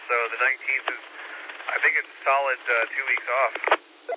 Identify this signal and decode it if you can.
Emergency Alert System (SAME) tones, heard as harsh buzzing bursts, with speech. It is heard from 0.3 to 3.2 s.